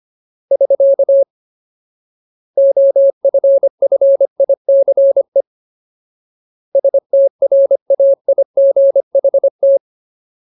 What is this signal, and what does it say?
Morse code, heard as a keyed tone, about 25 wpm, 550 Hz. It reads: <SK> OFFICE STRAIGHT
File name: Ultimate Word List / 25WPM Ultimate